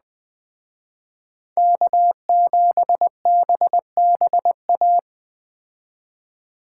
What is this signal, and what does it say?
Morse code, heard as a keyed tone, about 20 wpm, 700 Hz. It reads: K7BBA